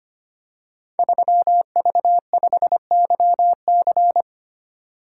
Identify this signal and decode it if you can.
Morse code, heard as a keyed tone, about 25 wpm, 700 Hz. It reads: 3V5YC
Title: Code Groups - 5 Letter:Numbers - 25wpm